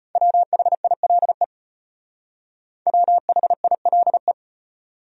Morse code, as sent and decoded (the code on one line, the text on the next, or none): .-- .... .. .-.. / .-- .... .. .-.. .
WHILE WHILE